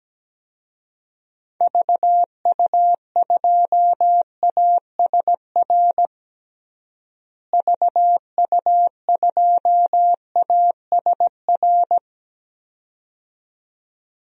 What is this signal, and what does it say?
Morse code, heard as a keyed tone, about 17 wpm, 700 Hz. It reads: VU2ASR VU2ASR
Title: DXCC-100 Call Signs - 6 Characters - Repeated 2x - 17wpm